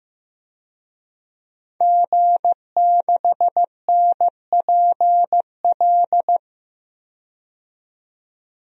Morse code, heard as a keyed tone, about 15 wpm, 700 Hz. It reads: G6NPL
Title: DXCC-100 Call Signs - 5 Characters - 15wpm